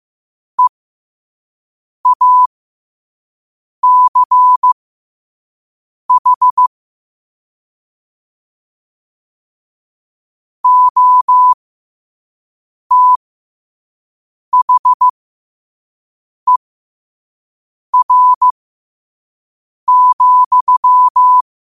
Morse code, heard as a keyed tone, about 15 wpm, spaced as for 5 wpm, 1000 Hz. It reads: EACH OTHER,
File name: qotdfiles / 5wpm